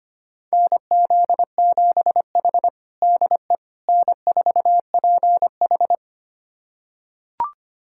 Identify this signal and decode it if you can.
Morse code, heard as a keyed tone, about 25 wpm, 700 Hz. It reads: NZ7H DE N4PH
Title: USA Callsign DE Callsign - 4 Characters - 25wpm